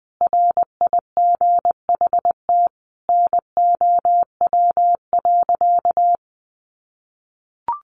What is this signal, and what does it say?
Morse code, heard as a keyed tone, about 20 wpm, 700 Hz. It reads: RIGHT NOW.
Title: Sentences from Top 700 Words 20wpm